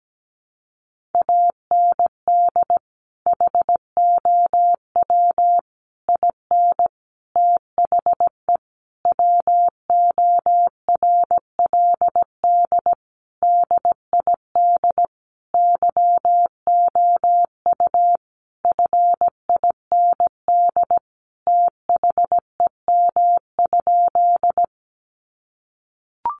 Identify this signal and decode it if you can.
Morse code, heard as a keyed tone, about 17 wpm, 700 Hz. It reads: AND HOW IN THE WORLD DID YOU FIND THEM?
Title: Sentences from Top 200 Words 17wpm